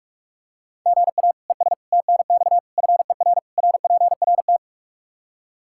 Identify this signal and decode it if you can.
Morse code, heard as a keyed tone, about 46 wpm, 700 Hz. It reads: GA ES TNX FER RPRT
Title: Practice QSO - Exchange 1 - Beginning - 45wpm